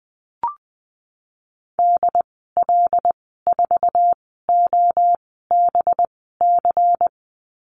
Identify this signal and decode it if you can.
Morse code, heard as a keyed tone, about 19 wpm, spaced as for 14 wpm, 700 Hz. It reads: DL4OBC
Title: 186 Lesson 30g - Letter b - Call Signs 20wpm